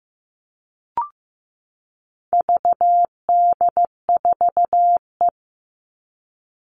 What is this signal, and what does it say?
Morse code, heard as a keyed tone, about 15 wpm, 700 Hz. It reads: VD4E
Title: All Cut Numbers - 4-Digit 15wpm